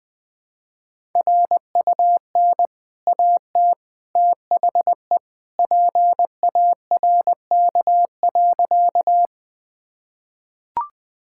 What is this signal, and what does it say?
Morse code, heard as a keyed tone, about 20 wpm, 700 Hz. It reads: RUN AT THE PARK.
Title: Easy Sentences with Hints - 20wpm